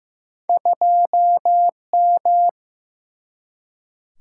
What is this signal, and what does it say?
Morse code, heard as a keyed tone, about 15 wpm, 700 Hz. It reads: 2M
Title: Single-Letter Single-Number 15wpm